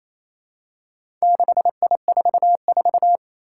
Morse code, heard as a keyed tone, about 27 wpm, 700 Hz. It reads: 6I44
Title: Four Challenging Letter-Number Combinations 28wpm